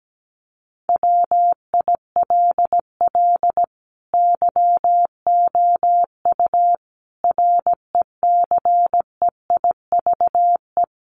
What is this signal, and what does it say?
Morse code, heard as a keyed tone, about 17 wpm, 700 Hz. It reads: WILL YOU RECEIVE
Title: Sentences from Top 1000 Words 17wpm